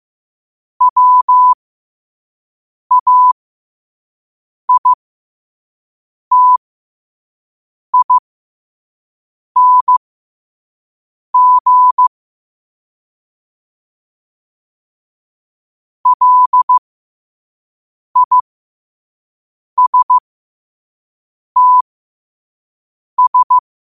Morse code, heard as a keyed tone, about 15 wpm, spaced as for 5 wpm, 1000 Hz. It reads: WAITING LISTS
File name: qotdfiles / 5wpm